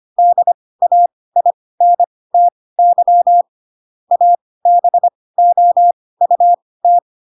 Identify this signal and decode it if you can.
Morse code, heard as a keyed tone, about 25 wpm, spaced as for 18 wpm, 700 Hz. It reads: DAINTY ABOUT
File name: Pinocchio / Pinocchio ch7 12 25 ik1hge